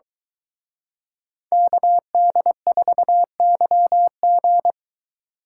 Morse code, heard as a keyed tone, about 23 wpm, 700 Hz. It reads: KD4YG